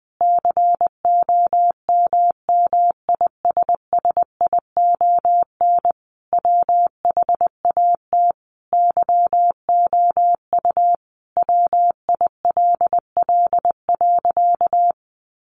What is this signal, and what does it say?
Morse code, heard as a keyed tone, about 20 wpm, 700 Hz. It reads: COMMISSION WHAT YOU WILL.
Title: Sentences from Top 1400 Words 20wpm